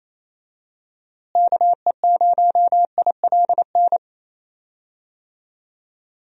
Morse code, heard as a keyed tone, about 28 wpm, 700 Hz. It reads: KE0ILN
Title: USA Call Signs - 6 Characters - Repeated 2x - 28wpm